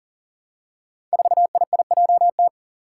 Morse code, heard as a keyed tone, about 40 wpm, 700 Hz. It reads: 4IIJT
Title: Five Challenging Letter-Number Combinations 40wpm